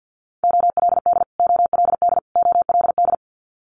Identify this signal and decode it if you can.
Morse code, heard as a keyed tone, about 50 wpm, 700 Hz. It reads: OLD OLD OLD
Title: Multi-letter Phonemes Repeated 3x 50wpm